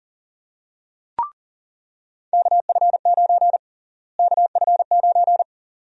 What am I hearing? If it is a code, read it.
Morse code, heard as a keyed tone, about 40 wpm, 700 Hz. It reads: KF9 KF9
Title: USA Call Sign Prefixes - Repeated 2x - 40wpm